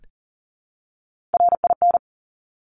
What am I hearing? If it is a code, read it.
Morse code, heard as a keyed tone, about 38 wpm, 690 Hz. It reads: RIN
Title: Top Three Letter Combinations in Words 40wpm